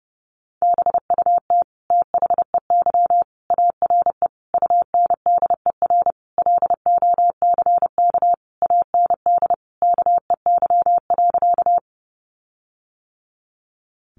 Morse code, heard as a keyed tone, about 30 wpm, 700 Hz. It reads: BUT THEY ARE UNDER LOCK AND KEY.
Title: Sentences from Top 1300 Words 30wpm